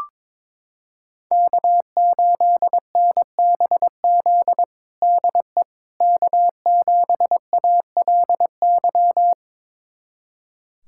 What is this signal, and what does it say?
Morse code, heard as a keyed tone, about 21 wpm, 700 Hz. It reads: K8NBZ DE K7ALY
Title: USA Callsign DE Callsign - 5 Characters - 22wpm